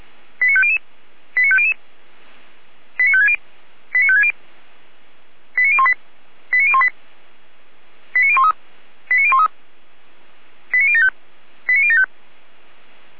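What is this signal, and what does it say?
Five-tone selective call (ZVEI), heard as quick runs of five tones, tones of about 70 ms. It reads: 89500 89500 89560 89560 89018 89018 89013 89013 89075 89075
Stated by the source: Five-tone ZVEI selective call received by SDR.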